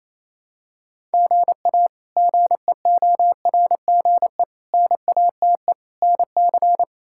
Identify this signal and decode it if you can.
Morse code, heard as a keyed tone, about 27 wpm, 700 Hz. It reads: GA GEORGE NATE NC
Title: K1USN SST QSO Practice - 28wpm